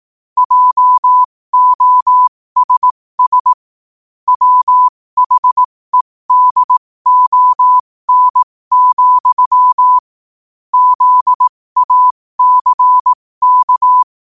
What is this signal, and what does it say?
Morse code, heard as a keyed tone, about 18 wpm, spaced as for 15 wpm, 1000 Hz. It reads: JOSS WHEDON, ZACK
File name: qotdfiles / 15wpm